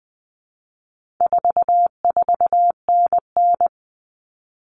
Morse code, heard as a keyed tone, about 20 wpm, 700 Hz. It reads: 44NN
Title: Common Cut Numbers - 4-Digit 20wpm